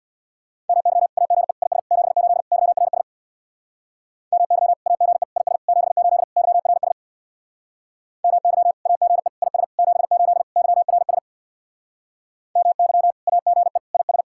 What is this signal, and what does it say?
Morse code, heard as a keyed tone, about 66 wpm, 700 Hz. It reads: MY AGE IS 78 YRS MY AGE IS 78 YRS MY AGE IS 78 YRS MY AGE IS